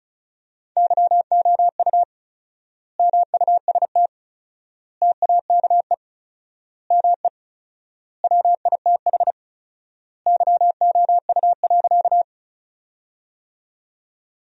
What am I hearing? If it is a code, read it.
Morse code, heard as a keyed tone, about 34 wpm, 700 Hz. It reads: YOU MUST TAKE ME WITH YOU.
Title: 4x Word-Spacing - Sentences from Top 200 Words - 35wpm